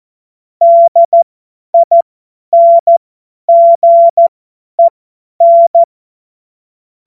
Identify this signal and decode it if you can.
Morse code, heard as a keyed tone, about 14 wpm, spaced as for 10 wpm, 680 Hz. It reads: DINGEN
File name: Morsekurs / Witze / Witz 11. Januar 2022, 50 F70